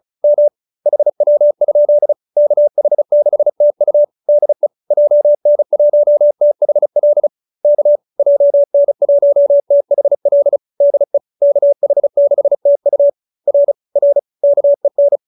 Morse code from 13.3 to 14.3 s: R R

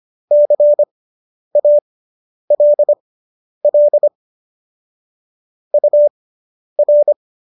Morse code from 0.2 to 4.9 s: CALL

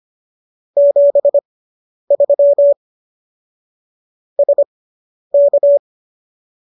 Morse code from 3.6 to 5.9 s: SK

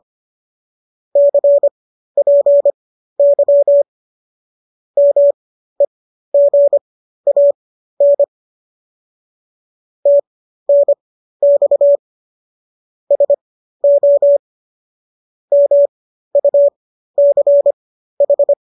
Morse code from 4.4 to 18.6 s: MEGAN TNX SO MUCH

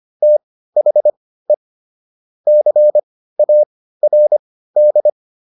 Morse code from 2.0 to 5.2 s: CARD

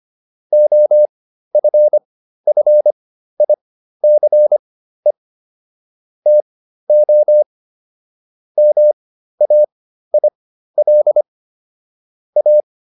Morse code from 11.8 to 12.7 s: A

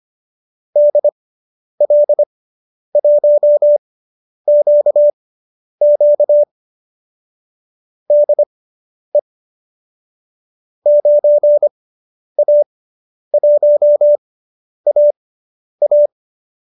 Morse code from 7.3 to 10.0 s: DE

